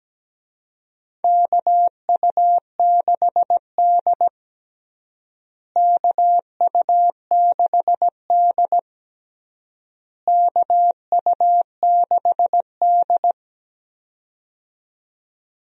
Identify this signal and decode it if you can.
Morse code, heard as a keyed tone, about 17 wpm, 700 Hz. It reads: KU6D KU6D KU6D
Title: USA Call Signs - 4 Characters - Repeated 3x - 17wpm